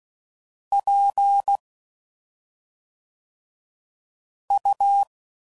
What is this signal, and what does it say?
Morse code, heard as a keyed tone, about 16 wpm, 800 Hz. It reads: P U
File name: Suffixes / Special ING's / SlowToFaster ING's